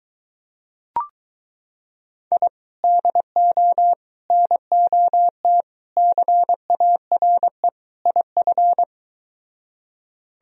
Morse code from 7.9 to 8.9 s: IF

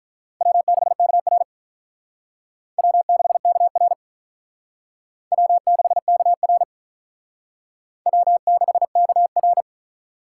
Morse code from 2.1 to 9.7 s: W6KR W6KR W6KR

